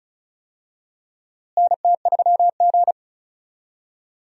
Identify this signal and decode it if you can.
Morse code, heard as a keyed tone, about 36 wpm, 700 Hz. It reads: NT3G
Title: USA Call Signs - 4 Characters - Repeated 2x - 35wpm